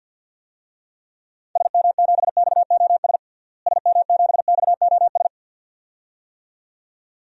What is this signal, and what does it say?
Morse code, heard as a keyed tone, about 49 wpm, 700 Hz. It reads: SM7XOS SM7XOS
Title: DXCC-100 Call Signs - 6 Characters - Repeated 2x - 50wpm